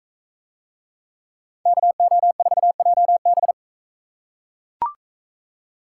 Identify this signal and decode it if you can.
Morse code, heard as a keyed tone, about 42 wpm, 700 Hz. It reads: KO4JB